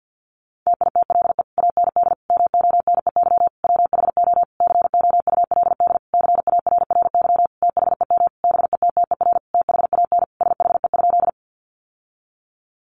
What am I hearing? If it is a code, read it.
Morse code, heard as a keyed tone, about 48 wpm, 700 Hz. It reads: TITLE AND MONEY WHO COULD CARRY THEM BETTER THAN SHE?
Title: Sentences from Top 1200 Words 50wpm